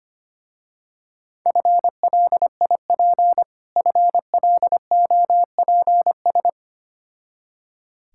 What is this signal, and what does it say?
Morse code, heard as a keyed tone, about 25 wpm, 700 Hz. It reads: FLIP FLOPS
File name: practice / 25wpm / Difficult Words with Hints - 25wpm